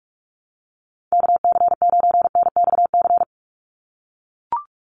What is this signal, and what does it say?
Morse code, heard as a keyed tone, about 44 wpm, 700 Hz. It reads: KC9NXC
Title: USA Call Signs 45wpm